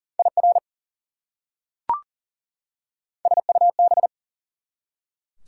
Morse code, heard as a keyed tone, about 40 wpm, 700 Hz. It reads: IR SUB
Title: Prefixes and Suffixes 40wpm